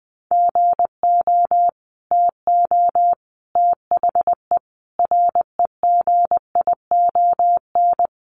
Morse code, heard as a keyed tone, about 20 wpm, 700 Hz. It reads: GO TO THE REGION
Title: Sentences from Top 1500 Words 20wpm